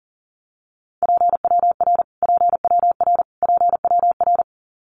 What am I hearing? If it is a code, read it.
Morse code, heard as a keyed tone, about 38 wpm, 700 Hz. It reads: PWR PWR PWR
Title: Abbreviations Repeated 3x - 40wpm